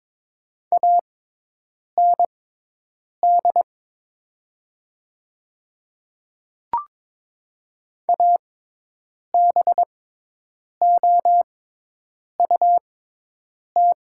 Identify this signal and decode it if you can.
Morse code, heard as a keyed tone, about 22 wpm, spaced as for 8 wpm, 700 Hz. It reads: AND ABOUT
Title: 6x Character Spacing - Top 100 Words - 22wpm